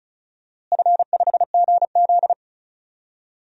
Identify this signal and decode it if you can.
Morse code, heard as a keyed tone, about 34 wpm, 700 Hz. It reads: F5GZ
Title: DXCC-100 Call Signs - 4 Characters - Repeated 2x - 35wpm